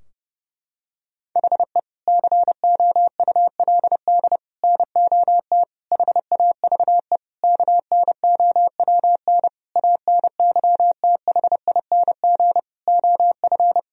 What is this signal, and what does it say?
Morse code, heard as a keyed tone, about 30 wpm, 700 Hz. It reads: HE COULD NOT HAVE KNOWN ANYTHING OF
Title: Sentences from Top 1000 Words - Review Part 3 - 30wpm